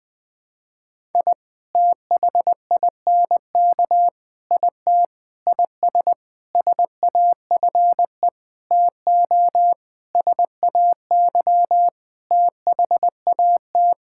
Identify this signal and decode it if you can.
Morse code, heard as a keyed tone, about 19 wpm, 700 Hz. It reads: I THINK IT IS SAFE TO SAY THAT
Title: Sentences with 7-Word Phrases - 20wpm